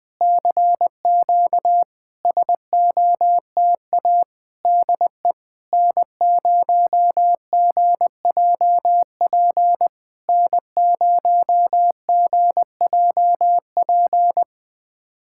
Morse code, heard as a keyed tone, about 20 wpm, 700 Hz. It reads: CQ SOTA DE N0GJP N0GJP
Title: SOTA QSO Practice 20wpm